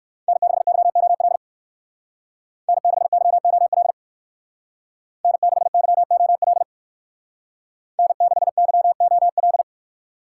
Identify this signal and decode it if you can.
Morse code, heard as a keyed tone, about 53 wpm, 700 Hz. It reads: N6YOL N6YOL N6YOL N6YOL
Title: Speed Racing - USA 1x3 Call Signs 45wpm